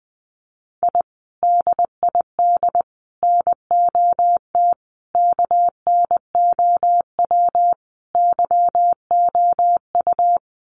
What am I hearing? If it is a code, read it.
Morse code, heard as a keyed tone, about 19 wpm, 700 Hz. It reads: I DID NOT KNOW YOU
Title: Sets of 5 Words Encore2 20wpm